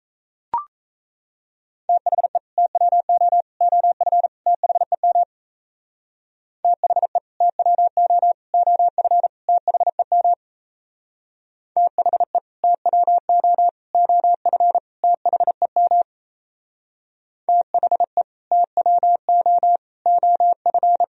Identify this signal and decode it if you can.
Morse code, heard as a keyed tone, about 35 wpm, 700 Hz. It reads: THE TWO OF THEM THE TWO OF THEM THE TWO OF THEM THE TWO OF
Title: Speed-Racing - Sets of 4 Words 28wpm